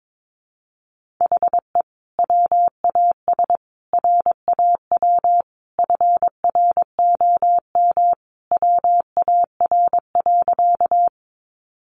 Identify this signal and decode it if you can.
Morse code, heard as a keyed tone, about 22 wpm, 700 Hz. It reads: HE WAS RAW FROM WAR.
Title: Sentences from Top 1500 Words 22wpm